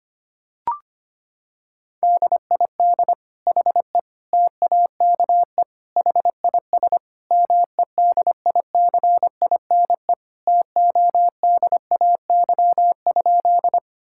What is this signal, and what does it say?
Morse code, heard as a keyed tone, about 25 wpm, 700 Hz. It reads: DID HE TAKE HIS MEDICINE TODAY?
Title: Sentences from Top 1000 Words - Review Part 6 - 25wpm